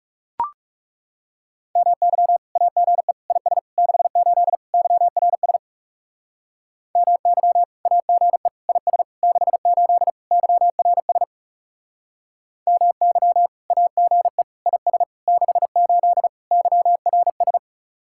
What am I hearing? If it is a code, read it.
Morse code, heard as a keyed tone, about 39 wpm, 700 Hz. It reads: MY AGE IS 68 YRS MY AGE IS 68 YRS MY AGE IS 68 YRS